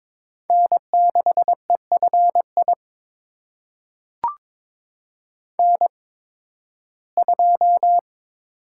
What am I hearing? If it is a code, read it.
Morse code, heard as a keyed tone, about 21 wpm, spaced as for 14 wpm, 700 Hz. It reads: N6EFI N2
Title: USA Call Signs - 5 Characters - 8x Character Spacing - 22wpm